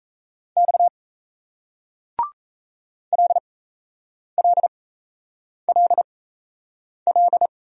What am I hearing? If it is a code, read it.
Morse code, heard as a keyed tone, about 38 wpm, spaced as for 8 wpm, 700 Hz. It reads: X LLLL